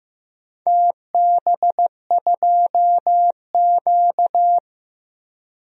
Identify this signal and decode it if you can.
Morse code, heard as a keyed tone, about 15 wpm, 700 Hz. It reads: TB2Q
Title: DXCC-100 Call Signs - 4 Characters - Repeated 3x - 15wpm